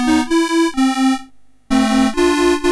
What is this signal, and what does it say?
There is no code.